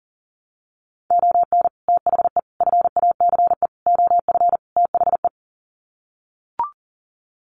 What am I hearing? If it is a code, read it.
Morse code, heard as a keyed tone, about 39 wpm, 700 Hz. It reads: ON THE FACE OF THE
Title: Sets of 5 Words Encore2 40wpm